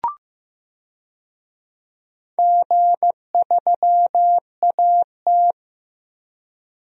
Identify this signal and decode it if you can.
Morse code, heard as a keyed tone, about 15 wpm, 700 Hz. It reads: G3AT
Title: DXCC-100 Call Signs - 4 Characters - 15wpm